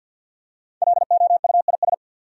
Morse code, heard as a keyed tone, about 52 wpm, 700 Hz. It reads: LOUIS
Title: USA Common Names - Max 5 Letters - 50wpm